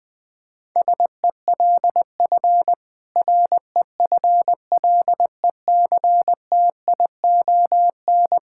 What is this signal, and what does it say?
Morse code, heard as a keyed tone, about 20 wpm, 700 Hz. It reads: SELF REFLECTION
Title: Difficult Sentences with Hints - 20wpm